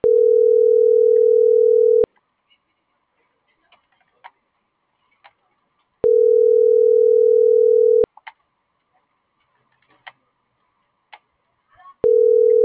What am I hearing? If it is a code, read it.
Ringback tone (North America), 440 + 480 Hz.